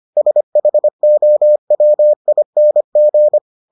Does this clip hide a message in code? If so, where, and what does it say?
Yes: Morse code from 0.2 to 3.4 s, heard as a keyed tone, about 25 wpm, 600 Hz. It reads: SHOWING